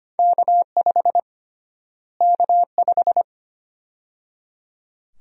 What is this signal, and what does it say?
Morse code, heard as a keyed tone, about 25 wpm, 700 Hz. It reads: K5 K5
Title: USA Call Sign Prefixes - Repeated 2x - 25wpm